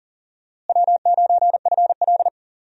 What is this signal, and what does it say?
Morse code, heard as a keyed tone, about 40 wpm, 700 Hz. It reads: W9FL
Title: USA Call Signs - 4 Characters - Repeated 2x - 40wpm